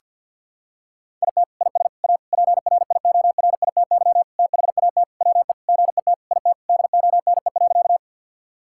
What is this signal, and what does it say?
Morse code, heard as a keyed tone, about 50 wpm, 700 Hz. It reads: IT IS A PRIORITY THAT WE GET IT DONE.
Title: Sentences from Top 1600 Words 50wpm